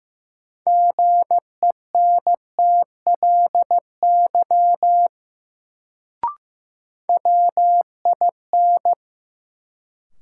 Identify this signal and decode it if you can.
Morse code, heard as a keyed tone, about 15 wpm, 700 Hz. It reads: GENTLY WIN
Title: Sentences from Top 1500 Words - New Word Drill Alt - 15wpm